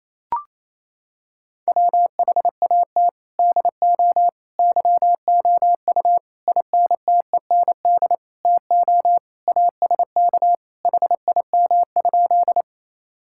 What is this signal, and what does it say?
Morse code, heard as a keyed tone, about 28 wpm, 700 Hz. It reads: WHAT DO YOU INTEND TO ASK HIM?
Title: Sentences from Top 1600 Words 28wpm